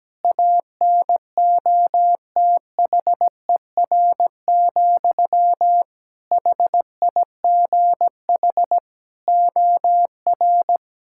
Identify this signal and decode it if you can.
Morse code, heard as a keyed tone, about 17 wpm, 700 Hz. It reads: ANOTHER, HIGH OR
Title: Sentences from Top 1600 Words 17wpm